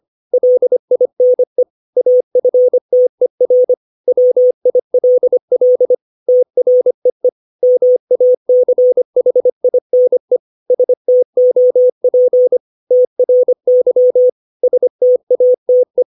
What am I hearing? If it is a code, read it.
Morse code, heard as a keyed tone, about 25 wpm, 500 Hz. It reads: LINE AFTER WILL TREE MACHINE STOP TRY STATE